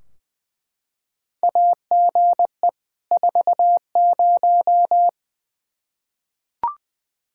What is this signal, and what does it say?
Morse code, heard as a keyed tone, about 20 wpm, 700 Hz. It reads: AGE 40
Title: Practice QSO - Exchange 3 - More Info - 20wpm